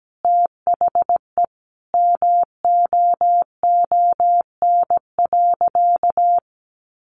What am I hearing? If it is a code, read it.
Morse code, heard as a keyed tone, about 17 wpm, 700 Hz. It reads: THE MOON.
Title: Sentences from Top 1300 Words 17wpm